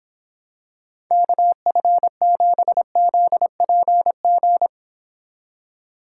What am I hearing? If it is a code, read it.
Morse code, heard as a keyed tone, about 26 wpm, 700 Hz. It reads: KF7ZPG